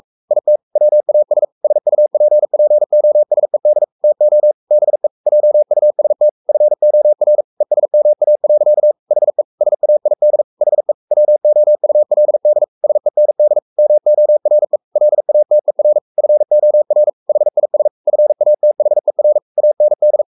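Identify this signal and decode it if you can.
Morse code, heard as a keyed tone, about 44 wpm, 600 Hz. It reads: IT WAS SUPPOSED TO BE JUST FOR ESMA. HE SAID HE WOULD SEND MORE LATER FOR HIS FATHER AND
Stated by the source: Morse code at 43 wpm sending Ken Follett's novel Never in English, from CW med Gnister.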